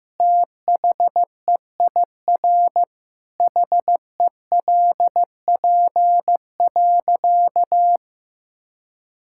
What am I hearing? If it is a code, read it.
Morse code, heard as a keyed tone, about 16 wpm, 700 Hz. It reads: THEIR HELP.
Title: Sentences from Top 1000 Words - Review Part 6 - 15wpm